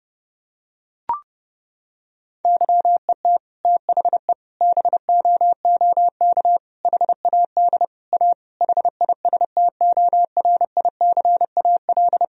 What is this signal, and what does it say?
Morse code, heard as a keyed tone, about 30 wpm, 700 Hz. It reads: YET THE BOOK HAD A HISTORICAL
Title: Sentences from Top 1000 Words - Review Part 4 - 30wpm